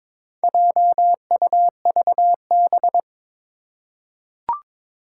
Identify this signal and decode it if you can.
Morse code, heard as a keyed tone, about 22 wpm, 700 Hz. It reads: JUVB
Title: Four Challenging Letter-Number Combinations 22wpm